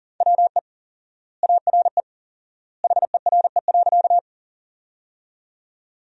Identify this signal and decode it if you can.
Morse code, heard as a keyed tone, about 40 wpm, 700 Hz. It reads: WE ARE HERE.
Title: 4x Word-Spacing - Sentences from Top 200 Words - 40wpm